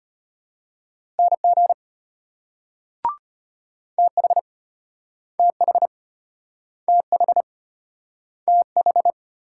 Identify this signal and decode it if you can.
Morse code, heard as a keyed tone, about 32 wpm, 700 Hz. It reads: NG TH TH TH TH